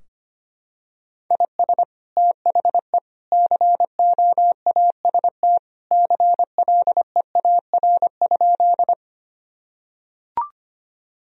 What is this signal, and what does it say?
Morse code, heard as a keyed tone, about 25 wpm, 700 Hz. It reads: IS THE COAST CLEAR?
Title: Sentences from Top 1000 Words - Review Part 4 - 25wpm